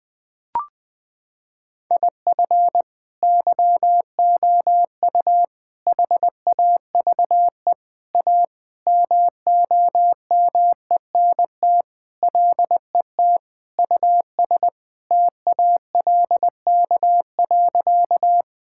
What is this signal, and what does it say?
Morse code, heard as a keyed tone, about 20 wpm, 700 Hz. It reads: IF YOU HAVE A MOMENT LET US TALK.
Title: Sentences with 4-Word Phrases - 20wpm